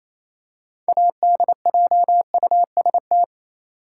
Morse code, heard as a keyed tone, about 28 wpm, 700 Hz. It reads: ADJUST
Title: Six Letter Words - 28wpm